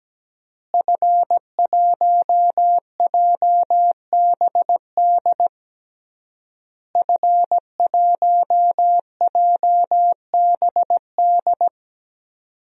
Morse code, heard as a keyed tone, about 17 wpm, 700 Hz. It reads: F1JBD F1JBD
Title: DXCC-100 Call Signs - 5 Characters - Repeated 3x - 17wpm